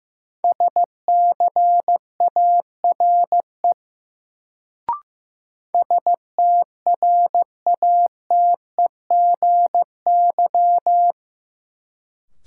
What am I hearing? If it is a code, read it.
Morse code, heard as a keyed tone, about 15 wpm, 700 Hz. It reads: SCARE STRATEGY
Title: Sentences from Top 1000 Words - Review Part 5 - New Word Drill - 15wpm